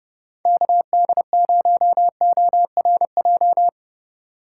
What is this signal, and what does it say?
Morse code, heard as a keyed tone, about 30 wpm, 700 Hz. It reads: KD0ORJ
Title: USA Call Signs - 6 Characters - 4x Character Spacing - 30wpm